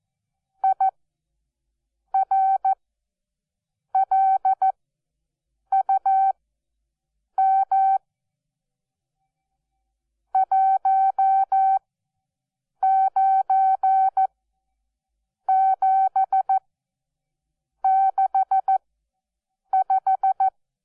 Morse code, heard as a keyed tone, about 14 wpm, spaced as for 6 wpm, 770 Hz. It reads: IRLUM 19765
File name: Onlinelehrgang / morsen / DJ4UF Morsen12